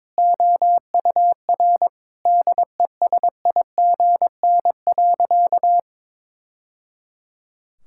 Morse code, heard as a keyed tone, about 22 wpm, 700 Hz. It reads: OUR DESIGN.
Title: Sentences from Top 1000 Words - Review Part 5 - 22wpm